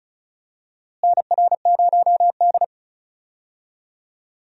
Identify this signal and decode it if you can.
Morse code, heard as a keyed tone, about 36 wpm, 700 Hz. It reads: NR0D
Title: USA Call Signs - 4 Characters - Repeated 3x - 35wpm